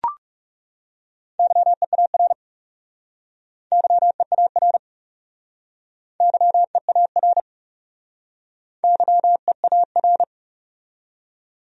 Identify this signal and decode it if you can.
Morse code, heard as a keyed tone, about 37 wpm, 700 Hz. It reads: YEAR YEAR YEAR YEAR
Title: Speed-Racing Top 200 Words 30wpm